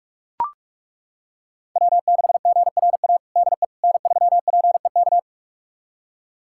Morse code, heard as a keyed tone, about 41 wpm, 700 Hz. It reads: W6GRA DE N3PEK
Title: USA Callsign DE Callsign 45wpm